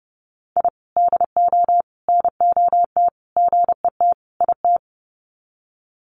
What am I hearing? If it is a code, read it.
Morse code, heard as a keyed tone, about 30 wpm, 700 Hz. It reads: I DO NOT GET IT
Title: Sets of 5 Words Encore 30wpm